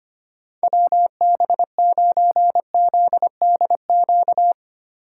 Morse code, heard as a keyed tone, about 31 wpm, 700 Hz. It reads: WB9ZDQ
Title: Speed Racing - USA Call Signs 25wpm